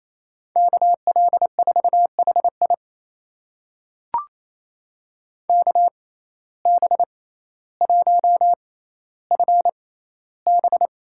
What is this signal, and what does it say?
Morse code, heard as a keyed tone, about 28 wpm, spaced as for 14 wpm, 700 Hz. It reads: KL4HI KB1FB
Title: USA Call Signs - 5 Characters - 6x Character Spacing - 28wpm